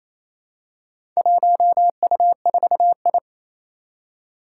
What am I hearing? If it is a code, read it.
Morse code, heard as a keyed tone, about 27 wpm, 700 Hz. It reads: 1U4I
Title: Four Challenging Letter-Number Combinations 28wpm